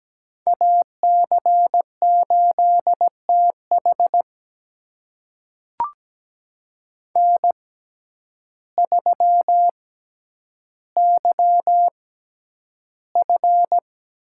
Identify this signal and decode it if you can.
Morse code, heard as a keyed tone, about 17 wpm, spaced as for 9 wpm, 700 Hz. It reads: AC8TH N3YF
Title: USA Call Signs - 6x Character Spacing - 17wpm